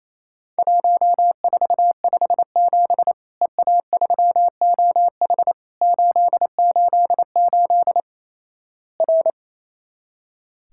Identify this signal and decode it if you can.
Morse code, heard as a keyed tone, about 28 wpm, 700 Hz. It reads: 1457 EA3OH 888 R
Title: Worked All Europe DX Contest - QTC Exchange - 28wpm